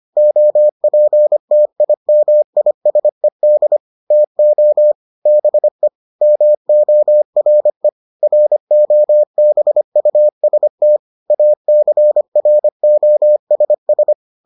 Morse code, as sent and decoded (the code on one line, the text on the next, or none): --- .--. - .. -- .. ... -.. / - --- / -... / -- --- .-. / .-. --- -... ..- ... - / .- -.-. .-. --- ... ...
OPTIMISED TO BE MORE ROBUST ACROSS